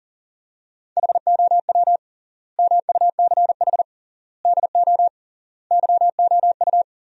Morse code, heard as a keyed tone, about 38 wpm, 700 Hz. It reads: HOW MUCH DO YOU
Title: 3x Word-Spacing - Sentences from Top 200 Words - 40wpm